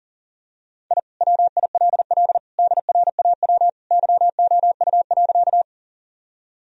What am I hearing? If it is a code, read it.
Morse code, heard as a keyed tone, about 37 wpm, 700 Hz. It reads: I WILL DRAW YOU.
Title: Sentences from Top 1000 Words - Review Part 1 - 40wpm